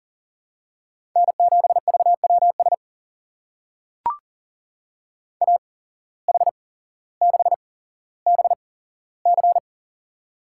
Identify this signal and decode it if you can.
Morse code, heard as a keyed tone, about 38 wpm, spaced as for 17 wpm, 700 Hz. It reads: N7VWS AH6BC
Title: USA Call Signs - 5 Characters - 8x Character Spacing - 40wpm